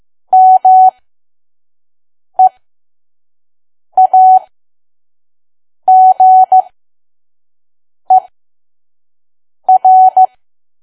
Morse code, heard as a keyed tone, about 15 wpm, spaced as for 5 wpm, 750 Hz. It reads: MEAGER
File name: Archive / 5 WPM files / PM